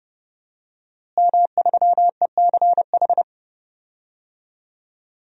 Morse code, heard as a keyed tone, about 30 wpm, 700 Hz. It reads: M3ECH
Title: DXCC-100 Call Signs - 5 Characters - Repeated 2x - 30wpm